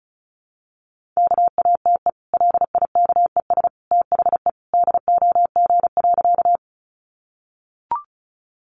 Morse code, heard as a keyed tone, about 34 wpm, 700 Hz. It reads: KATE LIKES THE DOG.
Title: Easy Phrases 35wpm